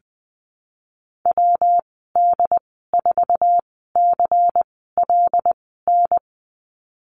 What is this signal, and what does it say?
Morse code, heard as a keyed tone, about 19 wpm, spaced as for 14 wpm, 700 Hz. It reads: WD4CLN
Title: 167 Lesson 27g - Number 7 - Call Signs 20wpm